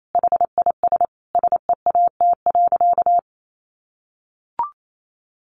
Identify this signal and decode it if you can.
Morse code, heard as a keyed tone, about 28 wpm, 700 Hz. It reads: HIS SEAT.
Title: Sentences from Top 1500 Words 28wpm